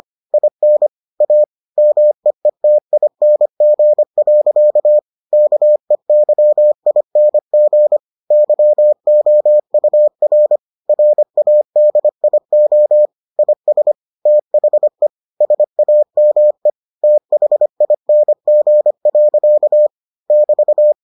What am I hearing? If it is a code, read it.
Morse code, heard as a keyed tone, about 25 wpm, 600 Hz. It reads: IN A MEETING. KEYING YOUR RADIO IS THE SAME THING. <BT>